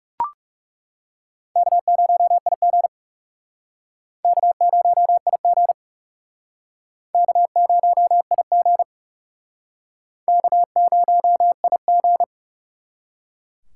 Morse code, heard as a keyed tone, about 38 wpm, 700 Hz. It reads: K0IG K0IG K0IG K0IG